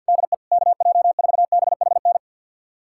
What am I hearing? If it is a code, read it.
Morse code, heard as a keyed tone, about 46 wpm, 700 Hz. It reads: DE KJ4BHN
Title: USA Callsign DE Callsign 50wpm